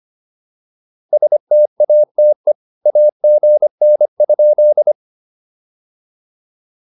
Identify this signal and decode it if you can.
Morse code, heard as a keyed tone, about 25 wpm, 590 Hz. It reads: STATE AGN?
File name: POTA209a 25